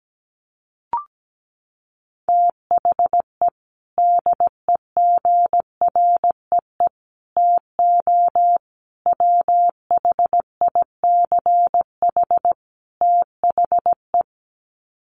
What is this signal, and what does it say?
Morse code, heard as a keyed tone, about 17 wpm, 700 Hz. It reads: THE DEGREE TO WHICH THE
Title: Sets of 5 Words Encore2 17wpm